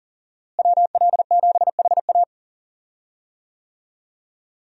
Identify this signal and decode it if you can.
Morse code, heard as a keyed tone, about 39 wpm, 700 Hz. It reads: WL7HA
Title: USA Call Signs - 5 Characters - Repeated 2x - 40wpm